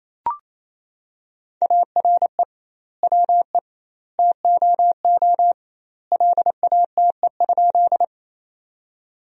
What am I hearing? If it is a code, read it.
Morse code, heard as a keyed tone, about 27 wpm, 700 Hz. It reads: ARE WE TOO LATE?
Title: 2x Word-Spacing - Sentences from Top 300 Words - 28wpm